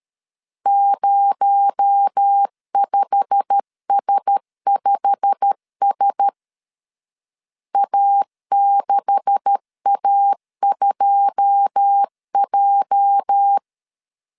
Morse code, heard as a keyed tone, about 13 wpm, 800 Hz. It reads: 05S5S A6A2J
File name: CW / DET / 1 9 1500C S B 12wpm 3819